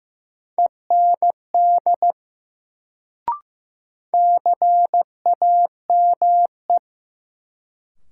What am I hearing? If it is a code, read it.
Morse code, heard as a keyed tone, about 15 wpm, 700 Hz. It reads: END CAME
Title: Sentences from Top 200 Words - New Word Drill - 15wpm